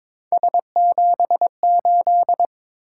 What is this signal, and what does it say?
Morse code, heard as a keyed tone, about 22 wpm, 700 Hz. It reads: S78
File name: practice / 22wpm / Three Letter-Number Combinations - 22wpm